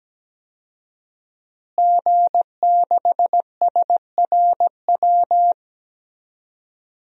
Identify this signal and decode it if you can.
Morse code, heard as a keyed tone, about 17 wpm, 700 Hz. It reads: G6SRW
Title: DXCC-100 Call Signs - 5 Characters - Repeated 2x - 17wpm